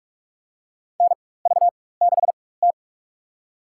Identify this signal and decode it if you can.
Morse code, heard as a keyed tone, about 46 wpm, spaced as for 21 wpm, 700 Hz. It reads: NV6T
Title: USA Call Signs - 4 Characters - 4x Character Spacing - 45wpm